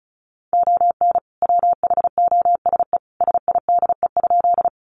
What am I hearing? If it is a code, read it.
Morse code, heard as a keyed tone, about 34 wpm, 700 Hz. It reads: ON WHOSE SIDE?
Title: Sentences from Top 1200 Words 35wpm